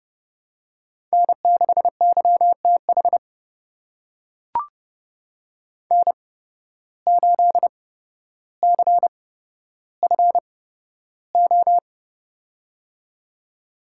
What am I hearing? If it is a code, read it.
Morse code, heard as a keyed tone, about 30 wpm, spaced as for 13 wpm, 700 Hz. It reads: N6YTH N8CFO
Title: USA Call Signs - 5 Characters - 8x Character Spacing - 30wpm